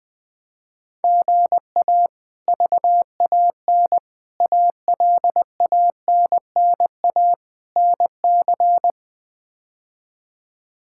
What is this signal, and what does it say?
Morse code, heard as a keyed tone, about 20 wpm, 700 Hz. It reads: GA VAN ALANNA NC
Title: K1USN SST QSO Practice - 20wpm